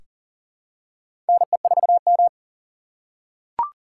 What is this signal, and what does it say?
Morse code, heard as a keyed tone, about 39 wpm, 700 Hz. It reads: NE4M
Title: K1USN SST QSO Practice - 40wpm